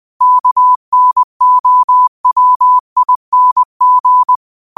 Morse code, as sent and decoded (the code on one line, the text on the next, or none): -.- -. --- .-- .. -. --.
KNOWING